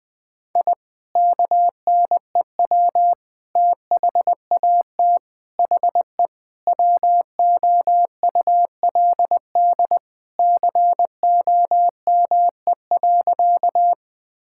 Morse code, heard as a keyed tone, about 21 wpm, 700 Hz. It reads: I KNEW THAT HE WOULD COME.
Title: Sentences with 2-Word Phrases - 20wpm